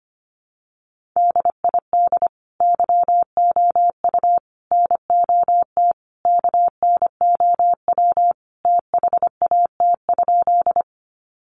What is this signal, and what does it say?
Morse code, heard as a keyed tone, about 25 wpm, 700 Hz. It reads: DID YOU NOT KNOW THAT?
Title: Sentences from Top 600 Words 25wpm